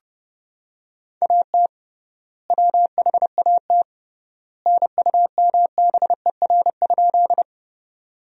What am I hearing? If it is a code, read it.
Morse code, heard as a keyed tone, about 32 wpm, 700 Hz. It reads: AT WHAT NUMBER?
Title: 3x Word-Spacing - Sentences from Top 100 Words - 30wpm